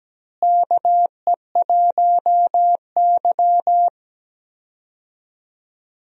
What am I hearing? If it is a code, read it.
Morse code, heard as a keyed tone, about 17 wpm, 700 Hz. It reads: KE1Y
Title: USA Call Signs - 4 Characters - Repeated 2x - 17wpm